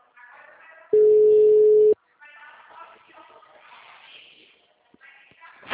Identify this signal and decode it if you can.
Ringback tone (Europe, 425 Hz).
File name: out- -2003- - - .354